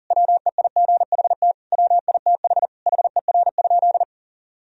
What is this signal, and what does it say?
Morse code, heard as a keyed tone, about 40 wpm, 700 Hz. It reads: WEIGHT WITH HER?
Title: Sentences from Top 1000 Words - Review Part 3 - 40wpm